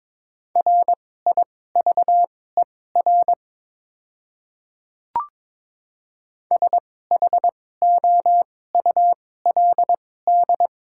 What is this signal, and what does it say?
Morse code, heard as a keyed tone, about 22 wpm, spaced as for 16 wpm, 700 Hz. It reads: RIVER SHOULD
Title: 2x Character Spacing - Top 300 Words - 22wpm